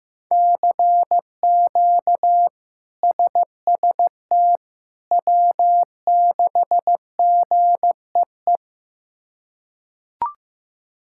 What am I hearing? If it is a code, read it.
Morse code, heard as a keyed tone, about 15 wpm, 700 Hz. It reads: CQ SST W6GEE
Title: K1USN SST QSO Practice - 15wpm